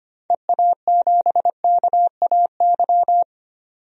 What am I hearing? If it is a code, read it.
Morse code, heard as a keyed tone, about 25 wpm, 700 Hz. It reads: EA7KAY
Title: DXCC-100 Call Signs - 6 Characters - 25wpm